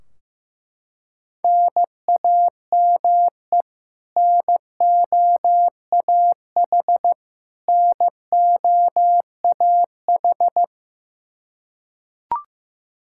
Morse code, heard as a keyed tone, about 15 wpm, 700 Hz. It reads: NAME NOAH NOAH
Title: Practice QSO - Exchange 1 - Beginning - 15wpm